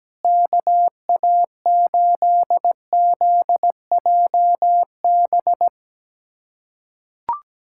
Morse code, heard as a keyed tone, about 17 wpm, 700 Hz. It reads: KA8ZJB
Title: USA Callsign DE Callsign - 6 Characters - 17wpm